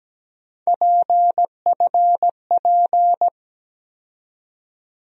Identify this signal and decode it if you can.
Morse code, heard as a keyed tone, about 17 wpm, 700 Hz. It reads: PFP
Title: USA Call Sign Postfixes - 3 Letters 17wpm